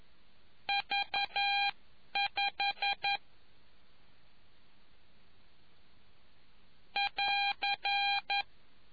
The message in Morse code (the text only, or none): V 5 <AR>